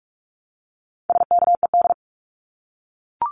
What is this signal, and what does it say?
Morse code, heard as a keyed tone, about 44 wpm, 700 Hz. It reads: SKED
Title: Abbreviations Repeated 3x - 45wpm